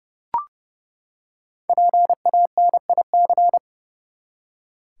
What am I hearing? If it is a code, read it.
Morse code, heard as a keyed tone, about 30 wpm, 700 Hz. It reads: PANIC
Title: Five Letter Words - 30wpm